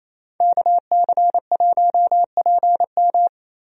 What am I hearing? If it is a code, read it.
Morse code, heard as a keyed tone, about 28 wpm, 700 Hz. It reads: KC1PM
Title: USA Call Signs - 5 Characters - Repeated 2x - 28wpm